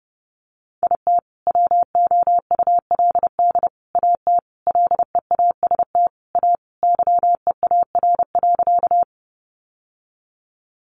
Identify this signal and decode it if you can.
Morse code, heard as a keyed tone, about 30 wpm, 700 Hz. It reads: IT WOULD AT LEAST A YEAR.
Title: Sentences from Top 700 Words 30wpm